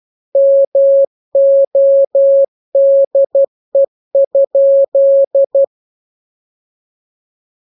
Morse code, heard as a keyed tone, about 12 wpm, 550 Hz. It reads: MODE?
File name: jokes / joke 27 12 0000